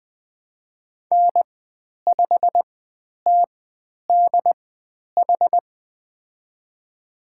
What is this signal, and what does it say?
Morse code, heard as a keyed tone, about 20 wpm, spaced as for 10 wpm, 700 Hz. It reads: N5TDH
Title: 095 Lesson 16g - Letter c - Call Signs 20wpm